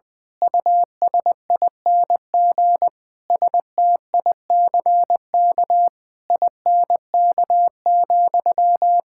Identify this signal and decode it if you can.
Morse code, heard as a keyed tone, about 20 wpm, 700 Hz. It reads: USING STICK INK,
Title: Pangram Challenge v2 - 20wpm